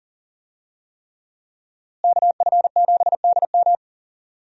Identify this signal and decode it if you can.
Morse code, heard as a keyed tone, about 40 wpm, 700 Hz. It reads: KF7DM
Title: USA Call Signs - 5 Characters - Repeated 3x - 40wpm